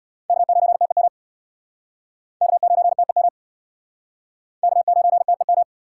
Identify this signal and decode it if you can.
Morse code, heard as a keyed tone, about 68 wpm, 700 Hz. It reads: K9TEM K9TEM K9TEM